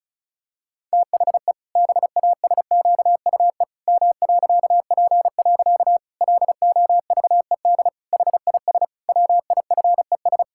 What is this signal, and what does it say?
Morse code, heard as a keyed tone, about 34 wpm, 700 Hz. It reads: THE BASQUE M.P. LOVED HIS WIFES
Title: Pangram Challenge v2 - 35wpm